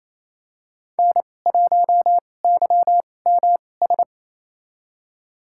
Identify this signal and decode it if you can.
Morse code, heard as a keyed tone, about 28 wpm, spaced as for 20 wpm, 700 Hz. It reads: N1YMS
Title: USA Call Signs - 2x Character Spacing - 28wpm